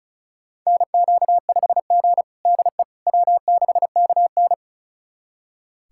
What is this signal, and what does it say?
Morse code, heard as a keyed tone, about 36 wpm, 700 Hz. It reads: NQ5G DE W6KN
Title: USA Callsign DE Callsign - 4 Characters - 35wpm